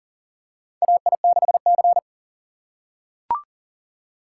Morse code, heard as a keyed tone, about 40 wpm, 690 Hz. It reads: AI6C
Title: USA Call Signs - 4 Characters - 2x Character Spacing - 40wpm